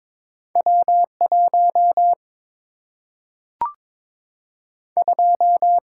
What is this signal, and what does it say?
Morse code, heard as a keyed tone, about 22 wpm, 700 Hz. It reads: W 1 2
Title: Single-Letter Single-Number 22wpm